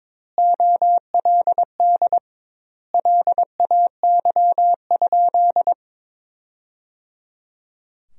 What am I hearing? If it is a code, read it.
Morse code, heard as a keyed tone, about 22 wpm, 700 Hz. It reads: OLD LAY?
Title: 2x Word-Spacing - Sentences from Top 400 Words - 22wpm